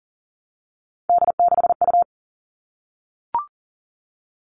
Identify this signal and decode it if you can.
Morse code, heard as a keyed tone, about 40 wpm, 700 Hz. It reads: D6U
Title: All Cut Numbers - 3-Digit 40wpm